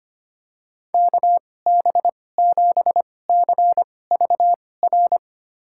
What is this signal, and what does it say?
Morse code, heard as a keyed tone, about 25 wpm, spaced as for 18 wpm, 700 Hz. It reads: KB7CVR
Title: USA Call Signs - 6 Characters - 2x Character Spacing - 25wpm